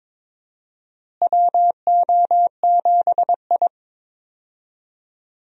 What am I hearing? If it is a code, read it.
Morse code, heard as a keyed tone, about 21 wpm, 700 Hz. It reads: WO7I
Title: USA Call Signs - 4 Characters - Repeated 3x - 22wpm